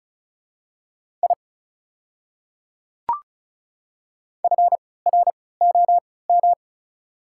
Morse code, heard as a keyed tone, about 34 wpm, spaced as for 20 wpm, 700 Hz. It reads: I FROM
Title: 3x Character Spacing - Top 300 Words - 35wpm